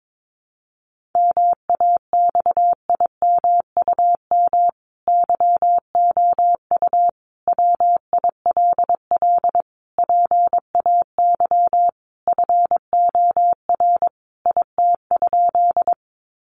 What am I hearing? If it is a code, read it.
Morse code, heard as a keyed tone, about 22 wpm, 700 Hz. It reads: MAXIMUM YOU WILL PAY FOR IT?
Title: Sentences from Top 1300 Words 22wpm